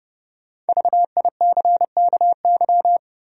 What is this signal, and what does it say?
Morse code, heard as a keyed tone, about 30 wpm, 700 Hz. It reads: VICKY
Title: USA Common Names - 30wpm